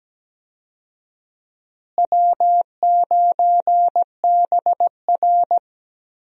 Morse code, heard as a keyed tone, about 17 wpm, 700 Hz. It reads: W9BR
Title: USA Call Signs - 4 Characters - Repeated 3x - 17wpm